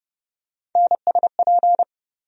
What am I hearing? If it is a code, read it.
Morse code, heard as a keyed tone, about 30 wpm, 700 Hz. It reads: NSP
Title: USA Call Sign Postfixes - 3 Letters 30wpm